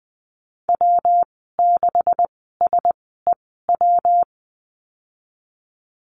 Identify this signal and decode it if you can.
Morse code, heard as a keyed tone, about 20 wpm, spaced as for 14 wpm, 700 Hz. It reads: W6SEW
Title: 219 Lesson 35g - Number 0 - Call Signs 20wpm